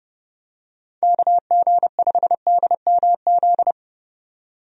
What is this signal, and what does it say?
Morse code, heard as a keyed tone, about 30 wpm, 700 Hz. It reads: KG5DMZ
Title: USA Call Signs - 6 Characters - 30wpm